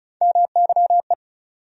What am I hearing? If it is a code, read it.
Morse code, heard as a keyed tone, about 36 wpm, 700 Hz. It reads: MYE
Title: USA Call Sign Postfixes - 3 Letters 35wpm